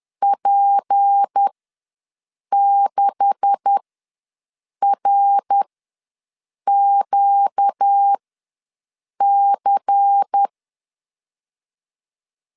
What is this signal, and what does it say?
Morse code, heard as a keyed tone, about 10 wpm, spaced as for 6 wpm, 800 Hz. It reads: P6RQC
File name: LEZIONI / 9 286 ws 10wpm 4032